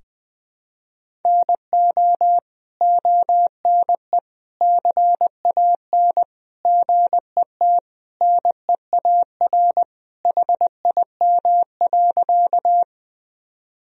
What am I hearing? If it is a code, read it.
Morse code, heard as a keyed tone, about 20 wpm, 700 Hz. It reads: NO ONE CAN GET NEAR HIM.
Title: Sentences from Top 1000 Words - Review Part 1 - 20wpm